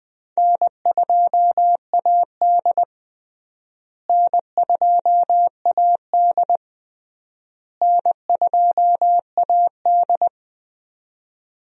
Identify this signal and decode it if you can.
Morse code, heard as a keyed tone, about 19 wpm, 700 Hz. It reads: N2AD N2AD N2AD
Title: USA Call Signs - 4 Characters - Repeated 3x - 20wpm